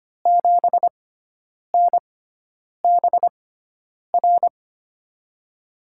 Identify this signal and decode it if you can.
Morse code, heard as a keyed tone, about 25 wpm, spaced as for 9 wpm, 700 Hz. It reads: 7NBR
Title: USA Call Signs - 6 Characters - 6x Character Spacing - 25wpm